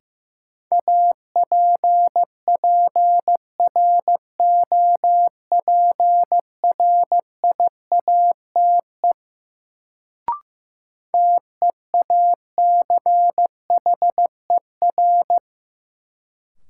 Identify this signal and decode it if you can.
Morse code, heard as a keyed tone, about 15 wpm, 700 Hz. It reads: APPROPRIATE TEACHER
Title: Sentences from Top 900 Words - New Word Drill - 15wpm